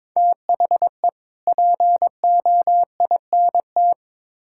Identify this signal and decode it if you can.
Morse code, heard as a keyed tone, about 22 wpm, 700 Hz. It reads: THE POINT
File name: practice / 22wpm / Build A Phrase - 5-Word Phrases - 22wpm